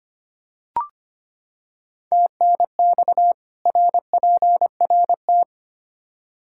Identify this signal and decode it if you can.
Morse code, heard as a keyed tone, about 25 wpm, 700 Hz. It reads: TNX RPRT
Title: Practice QSO - Exchange 2 - Info - 25wpm